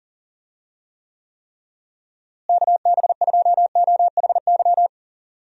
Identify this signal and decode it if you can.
Morse code, heard as a keyed tone, about 37 wpm, 700 Hz. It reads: KB2OHY
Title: USA Call Signs - 6 Characters - Repeated 3x - 40wpm